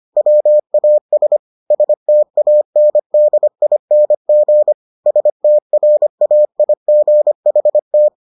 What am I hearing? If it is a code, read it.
Morse code, heard as a keyed tone, about 25 wpm, 600 Hz. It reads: WAS STANDING STRAIGHT